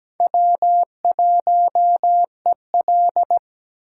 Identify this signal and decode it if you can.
Morse code, heard as a keyed tone, about 20 wpm, 700 Hz. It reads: W1EL
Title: Speed Racing - USA 1x2 Call Signs 17wpm